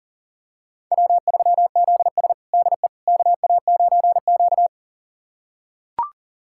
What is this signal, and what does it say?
Morse code, heard as a keyed tone, about 40 wpm, 700 Hz. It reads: W3ZS DE KA9Q
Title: USA Callsign DE Callsign - 4 Characters - 40wpm